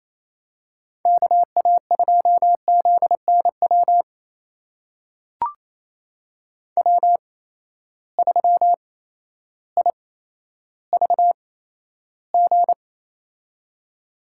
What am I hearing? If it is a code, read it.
Morse code, heard as a keyed tone, about 28 wpm, spaced as for 13 wpm, 700 Hz. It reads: KA2ZNW W3IVG
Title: USA Call Signs - 8x Character Spacing - 28wpm